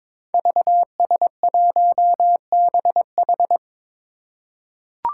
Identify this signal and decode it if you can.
Morse code, heard as a keyed tone, about 22 wpm, 700 Hz. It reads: VS1BH
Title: Five Challenging Letter-Number Combinations 22wpm